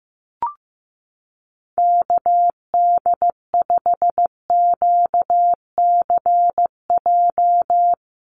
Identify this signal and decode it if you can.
Morse code, heard as a keyed tone, about 15 wpm, 700 Hz. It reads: KD5QCJ
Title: USA Call Signs Repeated 3x 15wpm